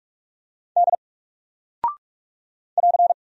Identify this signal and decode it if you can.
Morse code, heard as a keyed tone, about 45 wpm, 700 Hz. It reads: D <AR>
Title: Single Character (Letter:Number:Punctuation:Prosign) Alt 45wpm